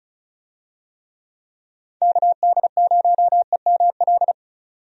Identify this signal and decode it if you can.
Morse code, heard as a keyed tone, about 36 wpm, 700 Hz. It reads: KD0EML
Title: USA Call Signs - 6 Characters - 35wpm